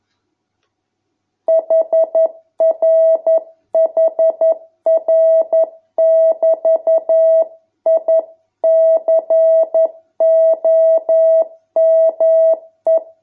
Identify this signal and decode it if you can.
Morse code, heard as a keyed tone, about 11 wpm, 650 Hz. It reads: HRHR<BT>I COME